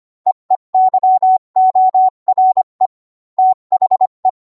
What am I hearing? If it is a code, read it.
Morse code, heard as a keyed tone, about 25 wpm, 750 Hz. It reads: EEYORE THE